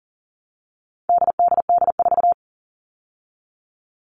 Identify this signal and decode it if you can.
Morse code, heard as a keyed tone, about 37 wpm, 700 Hz. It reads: DDD4
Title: All Cut Numbers - 4-Digit 40wpm